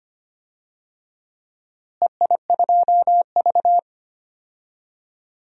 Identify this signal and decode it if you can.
Morse code, heard as a keyed tone, about 25 wpm, 700 Hz. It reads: EI2V
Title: DXCC-100 Call Signs - 4 Characters - Repeated 2x - 25wpm